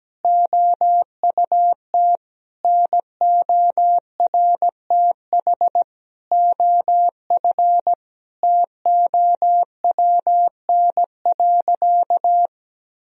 Morse code, heard as a keyed tone, about 17 wpm, 700 Hz. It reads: OUT NORTH OF TOWN.
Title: Sentences from Top 1000 Words - Review Part 1 - 17wpm